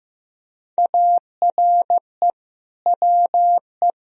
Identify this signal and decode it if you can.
Morse code, heard as a keyed tone, about 15 wpm, 700 Hz. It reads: ARE WE
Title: Sentences from Top 1000 Words - Review Part 3 - 15wpm